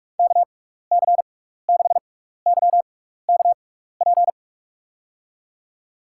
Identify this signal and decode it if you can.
Morse code, heard as a keyed tone, about 46 wpm, spaced as for 15 wpm, 700 Hz. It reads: KC6YKP
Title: USA Call Signs - 6 Characters - 6x Character Spacing - 45wpm